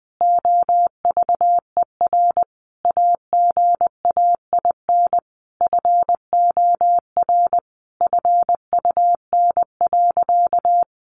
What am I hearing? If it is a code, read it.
Morse code, heard as a keyed tone, about 20 wpm, 700 Hz. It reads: OVER AGAIN FOR FUN.
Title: Easy Phrases 20wpm